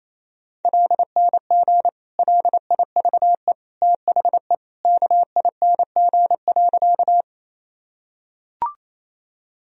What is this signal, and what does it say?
Morse code, heard as a keyed tone, about 28 wpm, 700 Hz. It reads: LNG LIVE THE KING.
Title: Sentences from Top 400 Words and Abbreviations 28wpm